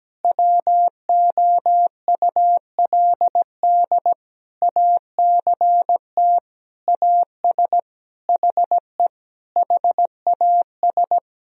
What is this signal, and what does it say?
Morse code, heard as a keyed tone, about 17 wpm, 700 Hz. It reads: WOULD ACT AS HE HAS
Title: Sentences from Top 1000 Words - Review Part 1 - 17wpm